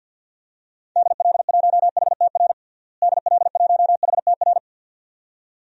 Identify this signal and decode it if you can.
Morse code, heard as a keyed tone, about 50 wpm, 700 Hz. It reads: DL1HTR DL1HTR
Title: DXCC-100 Call Signs - 6 Characters - Repeated 2x - 50wpm